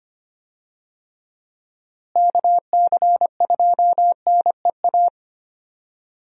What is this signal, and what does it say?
Morse code, heard as a keyed tone, about 25 wpm, 700 Hz. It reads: KC2NEA
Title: USA Call Signs - 6 Characters - Repeated 2x - 25wpm